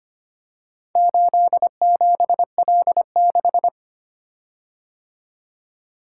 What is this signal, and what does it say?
Morse code, heard as a keyed tone, about 25 wpm, 700 Hz. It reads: 87L6
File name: practice / 25wpm / Four Letter-Number Combinations - 25wpm